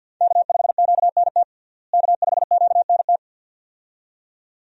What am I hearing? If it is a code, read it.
Morse code, heard as a keyed tone, about 52 wpm, 700 Hz. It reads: K5QNT K5QNT
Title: USA Call Signs - 5 Characters - Repeated 2x - 50wpm